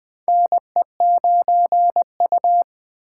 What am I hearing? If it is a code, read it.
Morse code, heard as a keyed tone, about 20 wpm, 700 Hz. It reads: NE9U
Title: K1USN SST QSO Practice - 20wpm